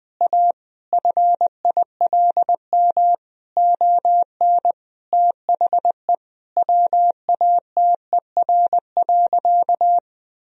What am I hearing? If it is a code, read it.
Morse code, heard as a keyed tone, about 20 wpm, 700 Hz. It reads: A FILM ON THE WATER.
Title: Sentences from Top 1000 Words - Review Part 3 - 20wpm